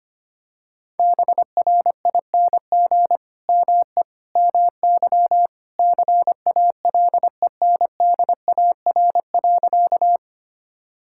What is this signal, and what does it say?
Morse code, heard as a keyed tone, about 25 wpm, 700 Hz. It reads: BRING ME MY CALENDAR.
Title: Sentences from Top 1600 Words 25wpm